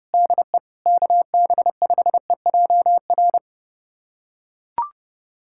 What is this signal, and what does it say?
Morse code, heard as a keyed tone, about 30 wpm, 700 Hz. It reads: DE KB5EJR
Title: USA Callsign DE Callsign - 6 Characters - 30wpm